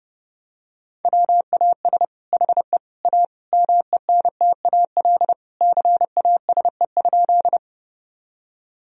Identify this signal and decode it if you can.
Morse code, heard as a keyed tone, about 32 wpm, 700 Hz. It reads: WAS HE A MENTAL CASE?
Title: Sentences from Top 1000 Words - Review Part 4 - 30wpm